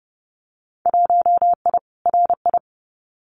Morse code, heard as a keyed tone, about 30 wpm, 700 Hz. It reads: 1I RI
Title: Winter Field Day Exchanges Encore 30wpm